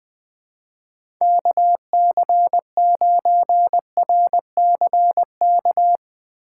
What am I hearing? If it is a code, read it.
Morse code, heard as a keyed tone, about 20 wpm, 700 Hz. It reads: KC9RCK
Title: USA Call Signs - 6 Characters - 4x Character Spacing - 20wpm